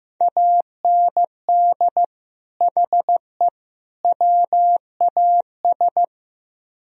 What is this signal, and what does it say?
Morse code, heard as a keyed tone, about 17 wpm, 700 Hz. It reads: AND HE WAS
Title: Speed-Racing - Sets of 3 Words 15wpm